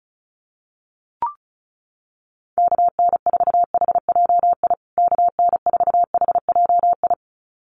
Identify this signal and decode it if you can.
Morse code, heard as a keyed tone, about 34 wpm, 700 Hz. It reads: KN4HJI KN4HJI
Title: USA Call Signs Repeated 2x 35wpm